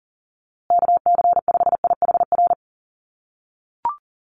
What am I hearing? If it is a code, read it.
Morse code, heard as a keyed tone, about 40 wpm, 700 Hz. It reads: KC5IHR
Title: USA Call Signs Repeated 3x 40wpm